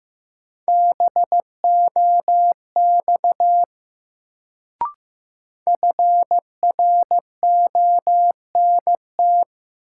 Morse code, heard as a keyed tone, about 15 wpm, 700 Hz. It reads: BOX FRONT
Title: Sentences from Top 500 Words - New Word Drill - 15wpm